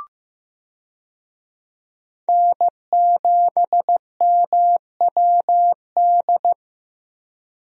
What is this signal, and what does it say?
Morse code, heard as a keyed tone, about 16 wpm, 700 Hz. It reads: N7MWD
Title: USA Call Signs - 5 Characters - 15wpm